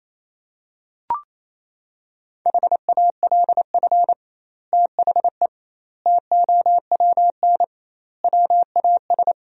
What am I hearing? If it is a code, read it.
Morse code, heard as a keyed tone, about 28 wpm, 700 Hz. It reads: HALF THE TOWN WAS
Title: 2x Word-Spacing - Sentences from Top 500 Words - 28wpm